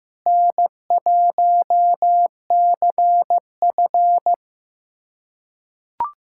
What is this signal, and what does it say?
Morse code, heard as a keyed tone, about 16 wpm, 700 Hz. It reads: N1CF
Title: USA Callsign DE Callsign - 4 Characters - 15wpm